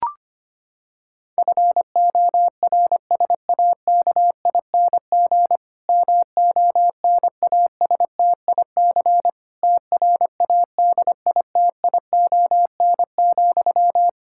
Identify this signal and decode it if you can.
Morse code, heard as a keyed tone, about 25 wpm, 700 Hz. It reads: FORSAKING MONASTIC TRADITION,
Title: Pangram Challenge v2 - 25wpm